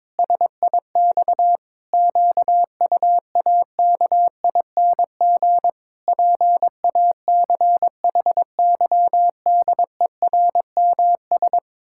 Morse code, heard as a keyed tone, about 22 wpm, 700 Hz. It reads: SIX QUAKING PACHYDERMS
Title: Pangram Challenge v2 - 22wpm